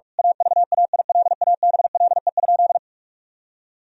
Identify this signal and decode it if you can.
Morse code, heard as a keyed tone, about 47 wpm, 700 Hz. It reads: AVAILABLE?
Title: Sentences from Top 1000 Words - Review Part 3 - 45wpm